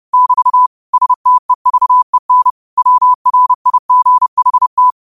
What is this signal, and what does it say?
Morse code, heard as a keyed tone, about 30 wpm, 1000 Hz. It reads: <BT> STEVEN WRIGHT